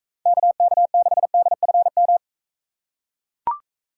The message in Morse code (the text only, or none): KK6DFM